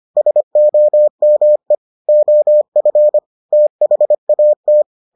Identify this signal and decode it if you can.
Morse code, heard as a keyed tone, about 25 wpm, 600 Hz. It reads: SOME OF THAT